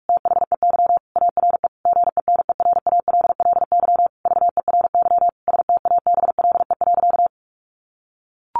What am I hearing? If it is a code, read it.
Morse code, heard as a keyed tone, about 44 wpm, 700 Hz. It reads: THEY ARE GENERALLY VERY STABLE.
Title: Sentences from Top 1500 Words 45wpm